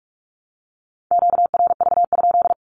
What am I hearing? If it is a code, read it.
Morse code, heard as a keyed tone, about 45 wpm, 700 Hz. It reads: QRV?
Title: Q-Codes Repeated 3x 45wpm